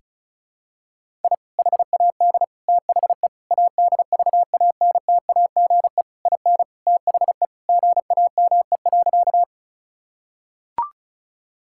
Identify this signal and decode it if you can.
Morse code, heard as a keyed tone, about 34 wpm, 700 Hz. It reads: I HAD THE ADVANTAGE IN THE GAME.
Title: Sentences from Top 1000 Words - Review Part 6 - 35wpm